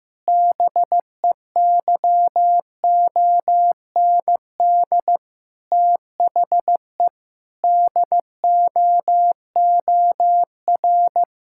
Morse code, heard as a keyed tone, about 16 wpm, 700 Hz. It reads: BEYOND THE DOOR
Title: Sentences from Top 1000 Words - Review Part 6 - 15wpm